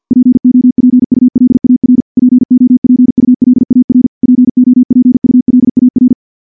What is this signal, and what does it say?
Morse code, heard as a keyed tone, about 50 wpm, 250 Hz. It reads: POPULAR POPULAR POPULAR
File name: morse / morse-training-50WPM